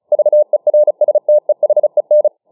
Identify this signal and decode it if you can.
Morse code, heard as a keyed tone, about 34 wpm, 600 Hz. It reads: VERSTEHEN